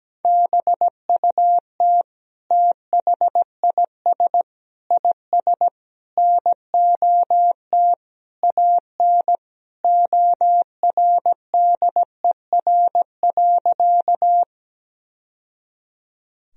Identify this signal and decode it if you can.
Morse code, heard as a keyed tone, about 17 wpm, 700 Hz. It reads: BUT THIS IS NOT AN ORDER.
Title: Sentences from Top 1000 Words - Review Part 2 - 17wpm